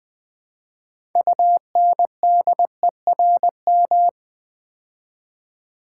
Morse code, heard as a keyed tone, about 20 wpm, 700 Hz. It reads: UNDERM
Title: Build A Word - 3 Letters At A Time - 9-Letter Words - 20wpm